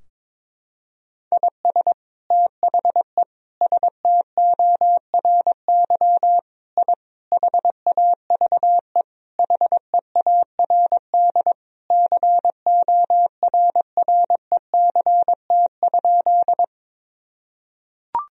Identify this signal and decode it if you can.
Morse code, heard as a keyed tone, about 22 wpm, 700 Hz. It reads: IS THE STORY I HAVE HEARD CORRECT?
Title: Sentences from Top 1000 Words - Review Part 2 - 22wpm